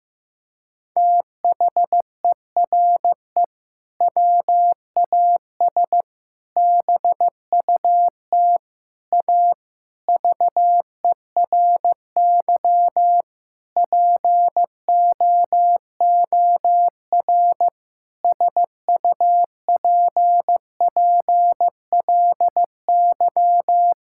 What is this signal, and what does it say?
Morse code, heard as a keyed tone, about 15 wpm, 700 Hz. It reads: THERE WAS BUT A VERY POOR SUPPLY
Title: Sentences from Top 1000 Words - Review Part 5 - 15wpm